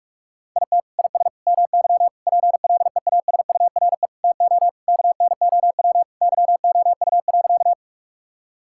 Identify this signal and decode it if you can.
Morse code, heard as a keyed tone, about 46 wpm, 700 Hz. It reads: IT IS MY PLEASURE TO KNOW YOU.
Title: Sentences from Top 1600 Words 45wpm